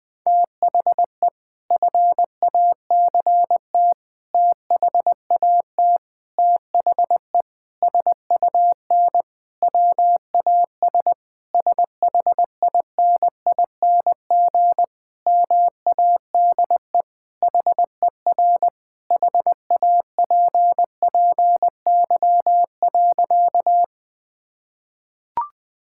Morse code, heard as a keyed tone, about 19 wpm, 700 Hz. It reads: THE FACT THAT THE SUN WAS SHINING MADE HER HAPPY.
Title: Sentences with 4-Word Phrases - Spoken Beginning - 20wpm